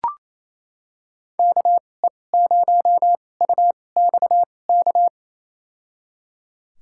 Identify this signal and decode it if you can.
Morse code, heard as a keyed tone, about 28 wpm, spaced as for 20 wpm, 700 Hz. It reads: KE0UXK
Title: USA Call Signs - 6 Characters - 2x Character Spacing - 28wpm